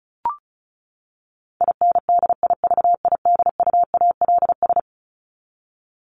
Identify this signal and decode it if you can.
Morse code, heard as a keyed tone, about 34 wpm, 700 Hz. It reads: INDIVIDUALS
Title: ES:S Suffix - Top 100 Words with Base Words 35wpm